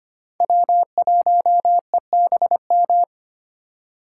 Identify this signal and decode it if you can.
Morse code, heard as a keyed tone, about 25 wpm, 700 Hz. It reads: W1EBM
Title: K1USN SST QSO Practice - 25wpm